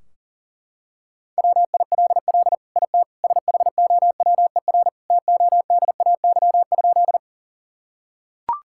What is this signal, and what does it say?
Morse code, heard as a keyed tone, about 37 wpm, 700 Hz. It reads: WILL IT SHOWER TODAY?
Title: Sentences from Top 1600 Words 40wpm